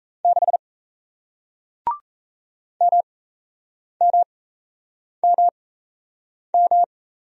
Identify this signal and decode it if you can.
Morse code, heard as a keyed tone, about 37 wpm, spaced as for 8 wpm, 700 Hz. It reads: 6 MMMM